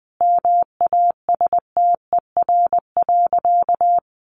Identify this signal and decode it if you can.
Morse code, heard as a keyed tone, about 20 wpm, 700 Hz. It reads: MASTER.
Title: Sentences from Top 1300 Words 20wpm